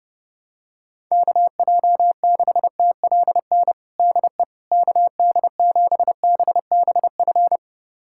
Morse code, heard as a keyed tone, about 31 wpm, 700 Hz. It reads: KJ6TLN DE KD7BBF
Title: USA Callsign DE Callsign - 6 Characters - 30wpm